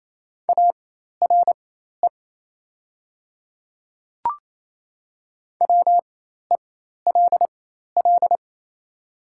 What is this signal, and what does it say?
Morse code, heard as a keyed tone, about 28 wpm, spaced as for 13 wpm, 700 Hz. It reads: ARE WELL
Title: 4x Character Spacing - Top 200 Words - 28wpm